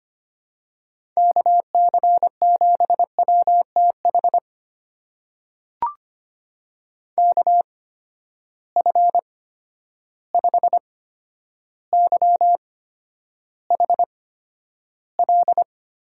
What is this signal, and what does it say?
Morse code, heard as a keyed tone, about 25 wpm, spaced as for 11 wpm, 700 Hz. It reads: KC7WTH KF5YHL
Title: USA Call Signs - 6 Characters - 8x Character Spacing - 25wpm